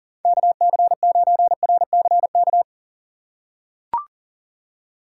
Morse code, heard as a keyed tone, about 40 wpm, 700 Hz. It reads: KC9RCK
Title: USA Call Signs - 6 Characters - 6x Character Spacing - 40wpm